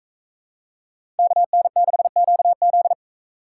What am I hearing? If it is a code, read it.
Morse code, heard as a keyed tone, about 38 wpm, 700 Hz. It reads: KN6QZ